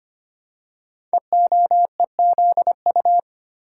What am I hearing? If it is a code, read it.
Morse code, heard as a keyed tone, about 25 wpm, 700 Hz. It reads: EOEZU
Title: Code Groups - 5 Letters - 25wpm